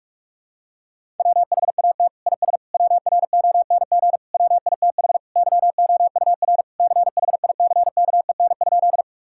Morse code, heard as a keyed tone, about 44 wpm, 700 Hz. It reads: WHAT IS WRONG WITH YOUR CHICKEN?
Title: Sentences from Top 1000 Words - Review Part 5 - 45wpm